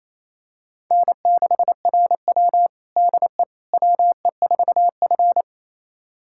Morse code, heard as a keyed tone, about 28 wpm, 700 Hz. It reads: N6RW DE WE4F
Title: USA Callsign DE Callsign - 4 Characters - 28wpm